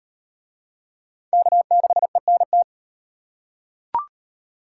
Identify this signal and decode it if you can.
Morse code, heard as a keyed tone, about 37 wpm, 700 Hz. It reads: K6ENT